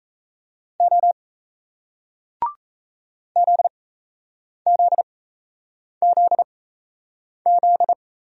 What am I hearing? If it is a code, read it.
Morse code, heard as a keyed tone, about 38 wpm, spaced as for 8 wpm, 700 Hz. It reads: O ZZZZ